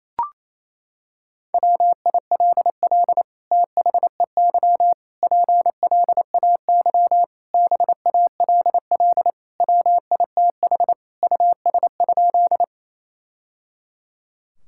Morse code, heard as a keyed tone, about 28 wpm, 700 Hz. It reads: WILL THEY PLAY BALL WITH US?
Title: Sentences from Top 1000 Words - Review Part 6 - 28wpm